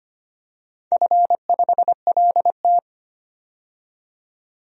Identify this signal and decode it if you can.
Morse code, heard as a keyed tone, about 25 wpm, 690 Hz. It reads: F5LT
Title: DXCC-100 Call Signs - 4 Characters - 25wpm